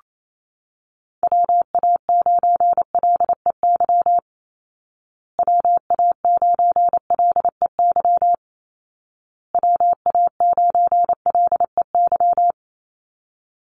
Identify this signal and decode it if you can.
Morse code, heard as a keyed tone, about 28 wpm, 700 Hz. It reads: WA9LEY WA9LEY WA9LEY
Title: Call Signs Repeated 2x - 28wpm